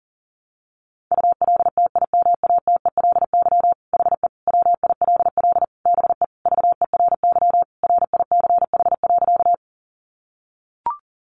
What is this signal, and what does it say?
Morse code, heard as a keyed tone, about 40 wpm, 700 Hz. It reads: ULTIMATELY HE WILL BE VERY RICH.
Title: Sentences from Top 1200 Words 40wpm